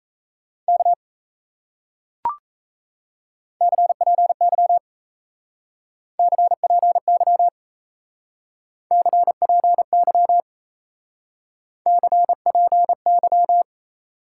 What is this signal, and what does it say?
Morse code, heard as a keyed tone, about 37 wpm, 700 Hz. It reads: K CPY CPY CPY CPY